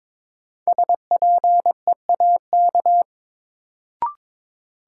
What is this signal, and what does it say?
Morse code, heard as a keyed tone, about 22 wpm, 700 Hz. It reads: SPEAK
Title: Sentences from Top 1000 Words - Review Part 2 - New Word Drill - 22wpm